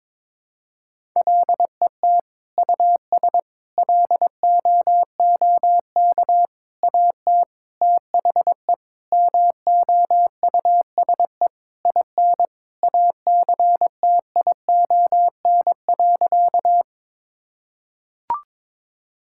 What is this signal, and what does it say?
Morse code, heard as a keyed tone, about 22 wpm, 700 Hz. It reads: LET US LOOK AT THE MOUSE IN ACTION.
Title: Sentences from Top 1000 Words - Review Part 5 - 22wpm